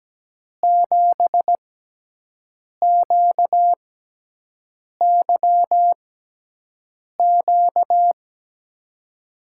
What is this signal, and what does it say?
Morse code, heard as a keyed tone, about 17 wpm, spaced as for 6 wpm, 700 Hz. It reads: 7QYQ
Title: USA Call Signs - 6x Character Spacing - 17wpm